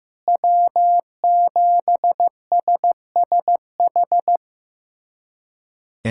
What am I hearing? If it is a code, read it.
Morse code, heard as a keyed tone, about 15 wpm, 700 Hz. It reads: W7SSH
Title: USA Callsign DE Callsign - 5 Characters - 15wpm